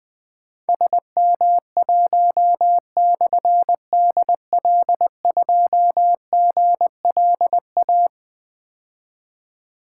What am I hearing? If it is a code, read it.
Morse code, heard as a keyed tone, about 20 wpm, 700 Hz. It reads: SM1/DL2GLA
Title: Difficult Call Signs 20wpm